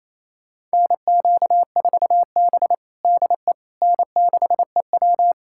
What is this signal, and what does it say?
Morse code, heard as a keyed tone, about 28 wpm, 700 Hz. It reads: NQ4B DE N6EW
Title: USA Callsign DE Callsign - 4 Characters - 28wpm